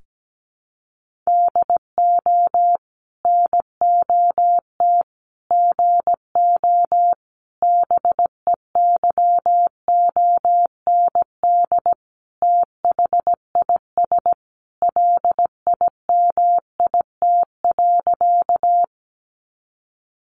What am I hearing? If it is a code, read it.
Morse code, heard as a keyed tone, about 17 wpm, 700 Hz. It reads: DO NOT GO BEYOND THIS LIMIT.
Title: Sentences from Top 1500 Words 17wpm